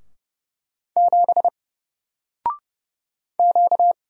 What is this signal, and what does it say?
Morse code, heard as a keyed tone, about 31 wpm, 700 Hz. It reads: 7 Q
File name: Single-Letter-Number-30wpm